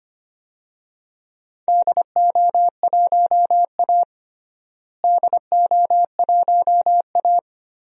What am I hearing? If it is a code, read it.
Morse code, heard as a keyed tone, about 25 wpm, 700 Hz. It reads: DO1A DO1A
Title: DXCC-100 Call Signs - 4 Characters - Repeated 3x - 25wpm